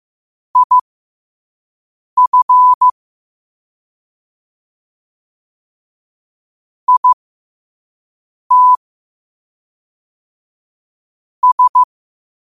Morse code, heard as a keyed tone, about 15 wpm, spaced as for 5 wpm, 1000 Hz. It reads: IF IT S
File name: qotdfiles / 5wpm